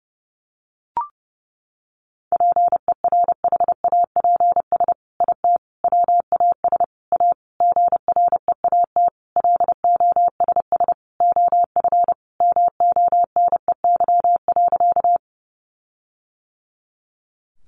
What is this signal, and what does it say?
Morse code, heard as a keyed tone, about 28 wpm, 700 Hz. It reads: PERHAPS IT WAS A GREAT LOSS OF MONEY.
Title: Sentences from Top 1400 Words 30wpm